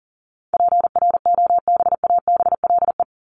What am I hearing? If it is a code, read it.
Morse code, heard as a keyed tone, about 40 wpm, 700 Hz. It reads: PROBABLE
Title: LY Suffix - Top 100 Words with Base Words 40wpm